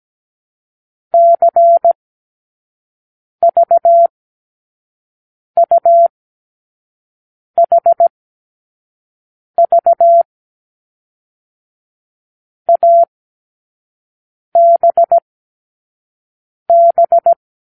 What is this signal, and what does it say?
Morse code, heard as a keyed tone, about 17 wpm, 680 Hz. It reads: C V U H V A B B
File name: k7qo / Lesson 36 (A-X)